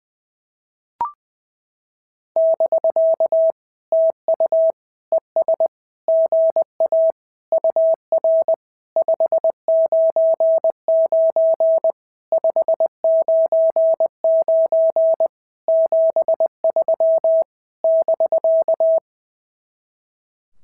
Morse code, heard as a keyed tone, about 20 wpm, 650 Hz. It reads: <BK> TU ES GA UR 599 599 73 <BK>
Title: SOTA QSO Practice 20wpm